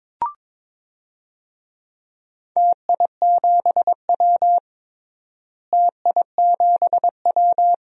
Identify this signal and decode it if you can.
Morse code, heard as a keyed tone, about 22 wpm, 700 Hz. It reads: TI7W TI7W
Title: DXCC-100 Call Signs - 4 Characters - Repeated 3x - 22wpm